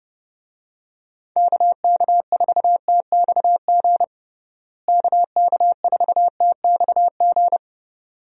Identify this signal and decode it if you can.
Morse code, heard as a keyed tone, about 32 wpm, 700 Hz. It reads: KK4TXG KK4TXG
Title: USA Call Signs - 6 Characters - Repeated 2x - 30wpm